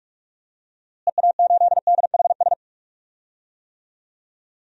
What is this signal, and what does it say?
Morse code, heard as a keyed tone, about 46 wpm, 700 Hz. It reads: EA8DHS
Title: DXCC-100 Call Signs - 6 Characters - Repeated 2x - 45wpm